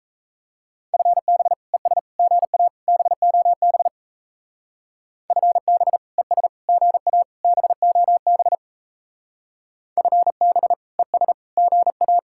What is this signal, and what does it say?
Morse code, heard as a keyed tone, about 34 wpm, 700 Hz. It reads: FB ES GA BOB FB ES GA BOB FB ES GA
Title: Speed-Racing Common QSO Phrases - 28wpm